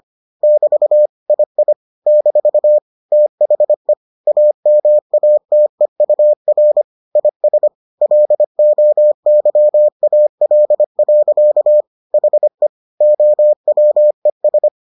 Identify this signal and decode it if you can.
Morse code, heard as a keyed tone, about 25 wpm, 600 Hz. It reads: <BT> II - THE AMATEUR IS LOYAL. HE OWES